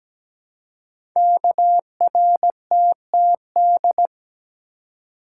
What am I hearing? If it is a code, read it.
Morse code, heard as a keyed tone, about 17 wpm, 700 Hz. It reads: KRTTD
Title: Code Groups - 5 Letter:Numbers - 17wpm